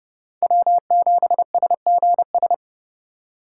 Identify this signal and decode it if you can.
Morse code, heard as a keyed tone, about 28 wpm, 700 Hz. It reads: W7SGS